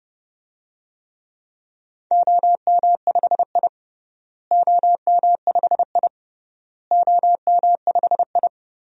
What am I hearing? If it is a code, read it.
Morse code, heard as a keyed tone, about 30 wpm, 700 Hz. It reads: OM5I OM5I OM5I
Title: DXCC-100 Call Signs - 4 Characters - Repeated 3x - 30wpm